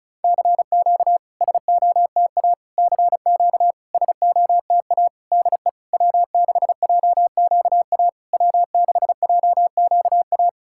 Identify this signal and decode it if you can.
Morse code, heard as a keyed tone, about 35 wpm, 690 Hz. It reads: CQ SOTA CQ SOTA DE W6JQA W6JQA
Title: SOTA QSO Practice 35wpm